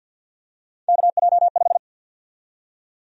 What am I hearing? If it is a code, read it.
Morse code, heard as a keyed tone, about 46 wpm, 700 Hz. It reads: KJ5
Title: USA Call Sign Prefixes - Repeated 2x - 50wpm